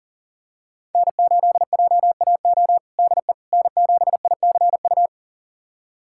Morse code, heard as a keyed tone, about 40 wpm, 700 Hz. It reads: N8JAO DE N7ICU
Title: USA Callsign DE Callsign 40wpm